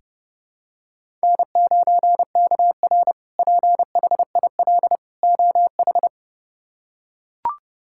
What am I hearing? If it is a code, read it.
Morse code, heard as a keyed tone, about 30 wpm, 700 Hz. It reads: N9KR PHIL OH
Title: K1USN SST QSO Practice - 30wpm